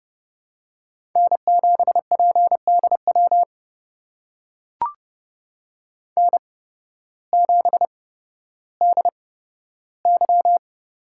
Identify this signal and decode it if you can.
Morse code, heard as a keyed tone, about 30 wpm, spaced as for 14 wpm, 700 Hz. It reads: N7PDW N7DY
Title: USA Call Signs - 5 Characters - 8x Character Spacing - 30wpm